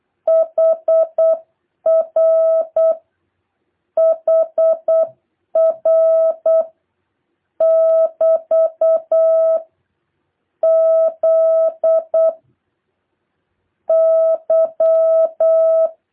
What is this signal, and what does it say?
Morse code, heard as a keyed tone, about 8 wpm, 660 Hz. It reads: HR HR <BT> Z Y